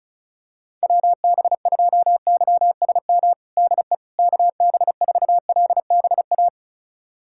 Morse code, heard as a keyed tone, about 36 wpm, 700 Hz. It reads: WB2YSM DE KB4LBA
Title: USA Callsign DE Callsign 35wpm